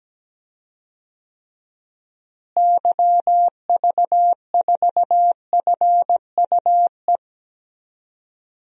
Morse code, heard as a keyed tone, about 17 wpm, 700 Hz. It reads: YV4FUE
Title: DXCC-100 Call Signs - 6 Characters - Repeated 2x - 17wpm